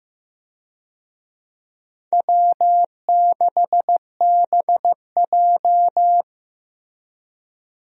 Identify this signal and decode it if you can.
Morse code, heard as a keyed tone, about 15 wpm, 700 Hz. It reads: W6BJ
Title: USA Call Signs - 4 Characters - Repeated 2x - 15wpm